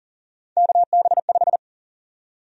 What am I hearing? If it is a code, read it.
Morse code, heard as a keyed tone, about 40 wpm, 700 Hz. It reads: KB5
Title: USA Call Sign Prefixes - 40wpm